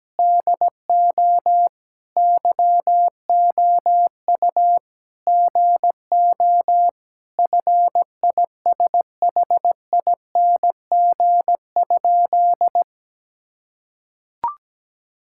Morse code, heard as a keyed tone, about 17 wpm, 700 Hz. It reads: DO YOU GO FISHING?
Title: Sentences from Top 1000 Words - Review Part 5 - 17wpm